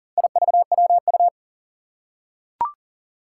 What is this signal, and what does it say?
Morse code, heard as a keyed tone, about 44 wpm, 700 Hz. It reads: IVWU
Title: Four Challenging Letter-Number Combinations 40wpm